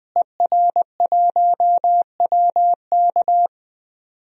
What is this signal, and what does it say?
Morse code, heard as a keyed tone, about 20 wpm, 700 Hz. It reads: ER1WK
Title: DXCC-100 Call Signs - 5 Characters - Repeated 3x - 20wpm